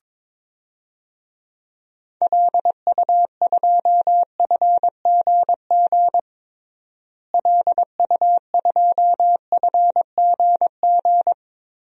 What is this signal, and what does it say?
Morse code, heard as a keyed tone, about 22 wpm, 700 Hz. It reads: LU2FGG LU2FGG
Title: DXCC-100 Call Signs - 6 Characters - Repeated 2x - 22wpm